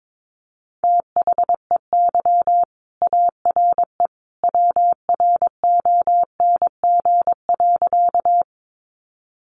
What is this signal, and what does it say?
Morse code, heard as a keyed tone, about 22 wpm, 700 Hz. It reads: THEY ARE WRONG.
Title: Sentences from Top 900 Words 22wpm